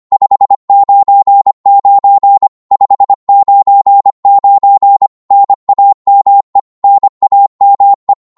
Morse code, heard as a keyed tone, about 25 wpm, 800 Hz. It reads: 599 599 NAME NAME
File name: Morse / qso-25wpm